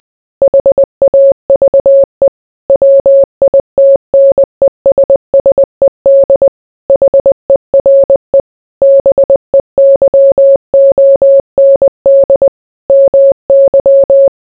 Morse code, heard as a keyed tone, about 20 wpm, 550 Hz. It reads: HAVE WITNESSED HERE BEYOND MY